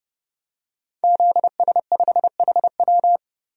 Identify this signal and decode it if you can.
Morse code, heard as a keyed tone, about 32 wpm, 700 Hz. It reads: ZS5HW
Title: DXCC-100 Call Signs - 5 Characters - Repeated 3x - 30wpm